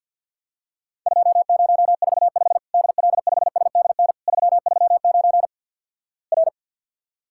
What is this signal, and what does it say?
Morse code, heard as a keyed tone, about 50 wpm, 700 Hz. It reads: 2045 DL5SDN 339 R
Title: Worked All Europe DX Contest - QTC Exchange with Repeats - 50wpm